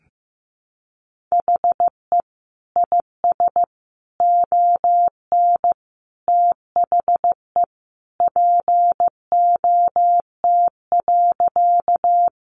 Morse code, heard as a keyed tone, about 15 wpm, 700 Hz. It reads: HE IS ON THE POT.
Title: Sentences from Top 1000 Words 15wpm